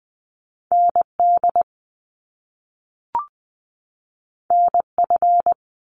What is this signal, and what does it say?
Morse code, heard as a keyed tone, about 20 wpm, 700 Hz. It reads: ND NF
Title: Top Two Letter Combinations in Words 20wpm